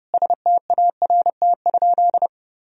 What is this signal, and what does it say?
Morse code, heard as a keyed tone, about 30 wpm, 700 Hz. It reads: START?
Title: Sentences from Top 1000 Words - Review Part 1 - 30wpm